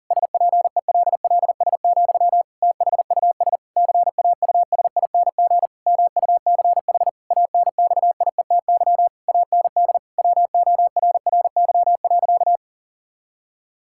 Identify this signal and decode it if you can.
Morse code, heard as a keyed tone, about 40 wpm, 700 Hz. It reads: SPELLS, THUS CAUSING MUCH ANXIETY AND WORRY.
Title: Pangram Challenge v2 - 40wpm